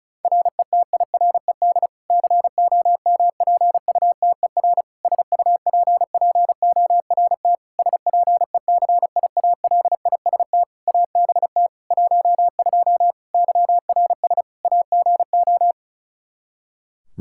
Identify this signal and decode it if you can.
Morse code, heard as a keyed tone, about 36 wpm, 700 Hz. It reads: RETIRED COMPUTER SUPPORT SPECIALIST ABT 12 YRS AGO
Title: Practice QSO - Exchange 3 - More Info - 35wpm